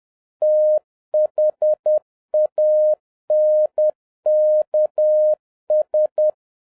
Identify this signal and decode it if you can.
Morse code, heard as a keyed tone, about 10 wpm, 610 Hz. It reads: THANKS
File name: 2011-07-mp3 / n36 THE 3.11 2 10wpm